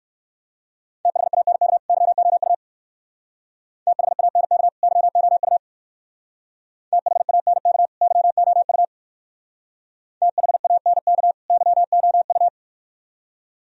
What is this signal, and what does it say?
Morse code, heard as a keyed tone, about 55 wpm, 700 Hz. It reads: THANK YOU THANK YOU THANK YOU THANK YOU
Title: Speed-Racing - Sets of 2 Words 45wpm